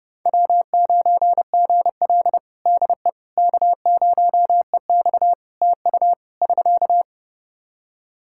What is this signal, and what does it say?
Morse code, heard as a keyed tone, about 32 wpm, 700 Hz. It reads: W9GL DE K0EX TU <SK>
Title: Practice QSO - Exchange 4 - Ending - 30wpm